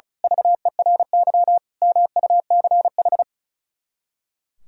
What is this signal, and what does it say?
Morse code, heard as a keyed tone, about 32 wpm, 700 Hz. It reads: VERY MUCH
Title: Speed-Racing - Sets of 5 Words 35wpm